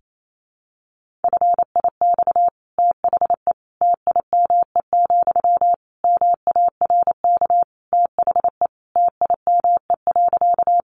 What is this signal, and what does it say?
Morse code, heard as a keyed tone, about 28 wpm, 700 Hz. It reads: FIX THE TIME, MARK THE TIME.
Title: Sentences from Top 1400 Words 28wpm